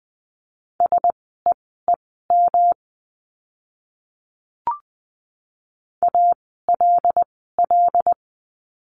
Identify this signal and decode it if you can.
Morse code, heard as a keyed tone, about 20 wpm, spaced as for 14 wpm, 700 Hz. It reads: SEEM ALL
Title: 244 Lesson 39f - Letter z - Words 20wpm